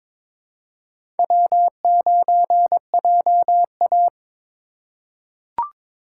Morse code, heard as a keyed tone, about 21 wpm, 690 Hz. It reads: W9JA
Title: USA Call Signs - 4 Characters - 3x Character Spacing - 22wpm